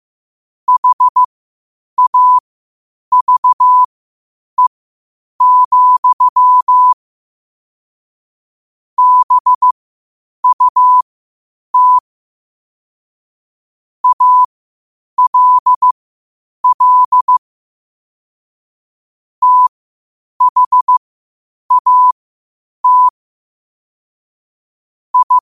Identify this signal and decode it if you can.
Morse code, heard as a keyed tone, about 15 wpm, spaced as for 8 wpm, 1000 Hz. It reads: HAVE, BUT ALL THAT I
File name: qotdfiles / 8wpm